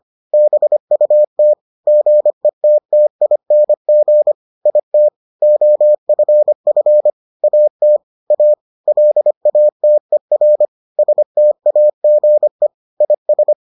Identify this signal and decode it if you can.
Morse code, heard as a keyed tone, about 25 wpm, 600 Hz. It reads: BUT GETTING IT OFF AT A LATER STAGE IS